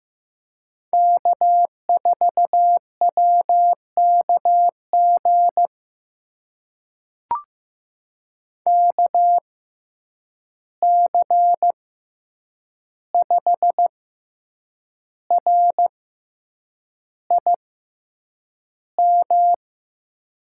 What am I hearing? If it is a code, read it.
Morse code, heard as a keyed tone, about 15 wpm, spaced as for 7 wpm, 700 Hz. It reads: K4WKG KC5RIM
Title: USA Call Signs - 6x Character Spacing - 15wpm